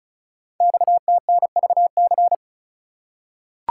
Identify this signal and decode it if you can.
Morse code, heard as a keyed tone, about 36 wpm, 690 Hz. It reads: XTNVC
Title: Code Groups - 5 Letter:Numbers - 35wpm